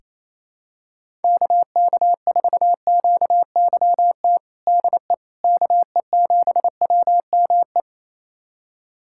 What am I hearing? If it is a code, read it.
Morse code, heard as a keyed tone, about 28 wpm, 700 Hz. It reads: KK4QYT DE KE7WME
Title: USA Callsign DE Callsign - 6 Characters - 28wpm